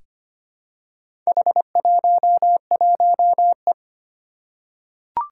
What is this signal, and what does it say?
Morse code, heard as a keyed tone, about 25 wpm, 700 Hz. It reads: H11E
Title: Four Challenging Letter-Number Combinations 25wpm